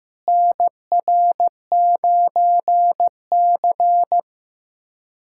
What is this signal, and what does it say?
Morse code, heard as a keyed tone, about 15 wpm, 700 Hz. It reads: NR9C
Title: USA Callsign DE Callsign - 4 Characters - 15wpm